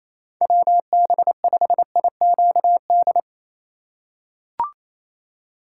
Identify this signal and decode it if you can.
Morse code, heard as a keyed tone, about 27 wpm, 700 Hz. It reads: WB5IQD
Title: USA Call Signs - 6 Characters - 2x Character Spacing - 28wpm